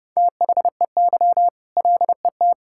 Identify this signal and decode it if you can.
Morse code, heard as a keyed tone, about 32 wpm, 700 Hz. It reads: THEY LET
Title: Sentences from Top 1000 Words - Review Part 5 - 30wpm